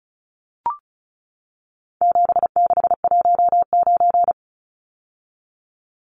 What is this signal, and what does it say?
Morse code, heard as a keyed tone, about 34 wpm, 700 Hz. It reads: 7619
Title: Numbers - 4-Digit 35wpm